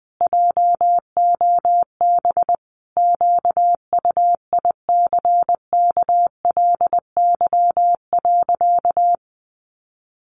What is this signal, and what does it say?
Morse code, heard as a keyed tone, about 20 wpm, 700 Hz. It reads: JOB QUICKLY.
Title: Sentences from Top 700 Words 20wpm